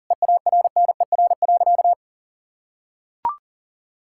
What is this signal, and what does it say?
Morse code, heard as a keyed tone, about 40 wpm, 700 Hz. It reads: EARNER.
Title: Pangram Challenge v2 - 40wpm